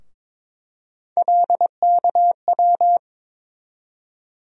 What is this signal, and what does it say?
Morse code, heard as a keyed tone, about 22 wpm, 700 Hz. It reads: LKW
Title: USA Call Sign Postfixes - 3 Letters 22wpm